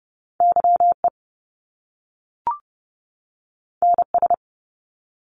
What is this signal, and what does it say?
Morse code, heard as a keyed tone, about 30 wpm, 700 Hz. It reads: YE NS
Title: Top Two Letter Combinations in Words 30wpm